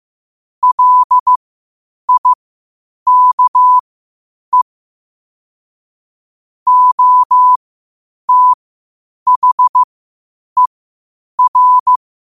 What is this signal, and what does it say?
Morse code, heard as a keyed tone, about 15 wpm, spaced as for 8 wpm, 1000 Hz. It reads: LIKE OTHER
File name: qotdfiles / 8wpm